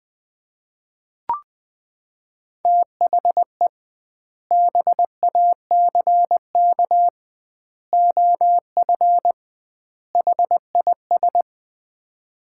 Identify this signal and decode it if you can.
Morse code, heard as a keyed tone, about 20 wpm, 700 Hz. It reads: THE BACK OF HIS
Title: 2x Word-Spacing - Sets of 4 Words - 20wpm